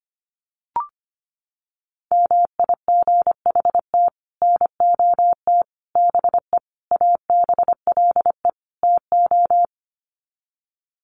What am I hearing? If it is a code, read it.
Morse code, heard as a keyed tone, about 25 wpm, 700 Hz. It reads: MIGHT NOT BE ABLE TO
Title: Sets of 5 Words Encore2 25wpm